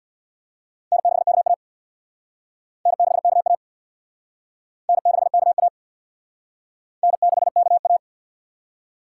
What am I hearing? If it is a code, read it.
Morse code, heard as a keyed tone, about 59 wpm, 700 Hz. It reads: N6KA N6KA N6KA N6KA
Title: Speed Racing - USA 1x2 Call Signs 50wpm